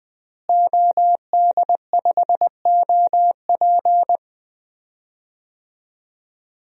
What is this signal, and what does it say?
Morse code, heard as a keyed tone, about 20 wpm, 700 Hz. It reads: OD5OP
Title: DXCC-100 Call Signs - 5 Characters - Repeated 2x - 20wpm